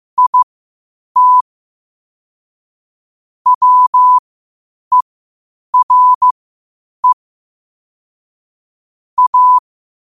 Morse code, heard as a keyed tone, about 15 wpm, spaced as for 8 wpm, 1000 Hz. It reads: IT WERE A
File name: qotdfiles / 8wpm